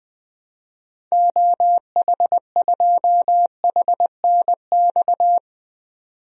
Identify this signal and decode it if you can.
Morse code, heard as a keyed tone, about 20 wpm, 700 Hz. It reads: OH2HNX
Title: DXCC-100 Call Signs - 6 Characters - 20wpm